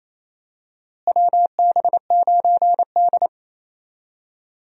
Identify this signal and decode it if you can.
Morse code, heard as a keyed tone, about 28 wpm, 700 Hz. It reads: WB9D
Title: USA Call Signs - 4 Characters - Repeated 2x - 28wpm